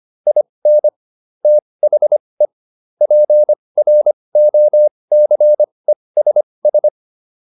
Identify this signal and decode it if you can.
Morse code, heard as a keyed tone, about 25 wpm, spaced as for 20 wpm, 600 Hz. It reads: IN THE PROCESS